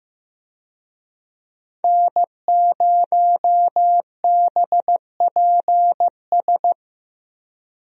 Morse code, heard as a keyed tone, about 14 wpm, 700 Hz. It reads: N0BPS
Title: USA Call Signs - 5 Characters - Repeated 2x - 15wpm